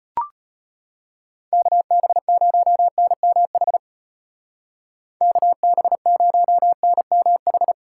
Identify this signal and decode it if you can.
Morse code, heard as a keyed tone, about 30 wpm, 700 Hz. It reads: KB0NMH KB0NMH
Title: Speed Racing - USA 2x3 Call Signs 25wpm